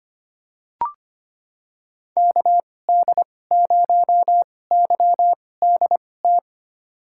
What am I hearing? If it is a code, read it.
Morse code, heard as a keyed tone, about 25 wpm, spaced as for 18 wpm, 700 Hz. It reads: KD0YDT
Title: USA Call Signs - 6 Characters - 2x Character Spacing - 25wpm